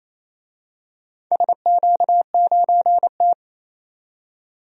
Morse code, heard as a keyed tone, about 28 wpm, 700 Hz. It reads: SQ9T
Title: DXCC-100 Call Signs - 4 Characters - 28wpm